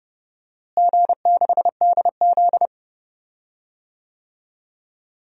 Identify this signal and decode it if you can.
Morse code, heard as a keyed tone, about 30 wpm, 700 Hz. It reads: G6DZ
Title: DXCC-100 Call Signs - 4 Characters - Repeated 3x - 30wpm